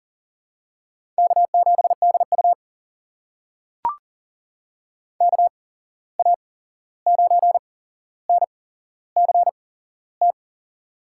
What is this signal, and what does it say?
Morse code, heard as a keyed tone, about 38 wpm, spaced as for 15 wpm, 700 Hz. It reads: K7DU KA9NCT
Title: USA Call Signs - 8x Character Spacing - 40wpm